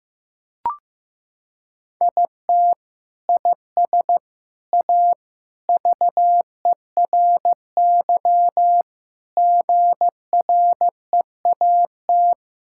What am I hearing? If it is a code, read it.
Morse code, heard as a keyed tone, about 15 wpm, 700 Hz. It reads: IT IS A VERY GREAT
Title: Sentences from Top 1000 Words - Review Part 5 - 15wpm